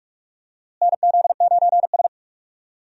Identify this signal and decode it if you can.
Morse code, heard as a keyed tone, about 46 wpm, 700 Hz. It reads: NZ9S
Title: USA Call Signs - 4 Characters - Repeated 3x - 45wpm